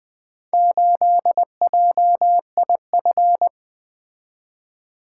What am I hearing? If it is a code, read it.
Morse code, heard as a keyed tone, about 20 wpm, 700 Hz. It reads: 8JIF
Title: Four Letter-Number Combinations - 20wpm